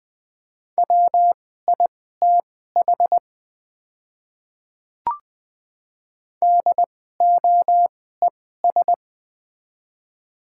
Morse code, heard as a keyed tone, about 20 wpm, spaced as for 15 wpm, 700 Hz. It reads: WITH DOES
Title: 2x Character Spacing - Top 200 Words - 20wpm